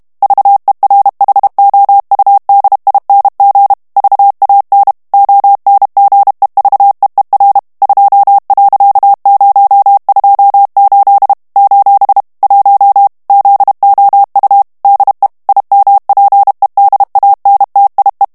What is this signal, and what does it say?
Morse code, heard as a keyed tone, about 32 wpm, 800 Hz. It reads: VERHOUDING VAN ONGEVEER 2.028 : 1 ZOU DE IMPEDANTIE
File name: HOLLANDAIS texte clair 1 vitesse 30 mots minute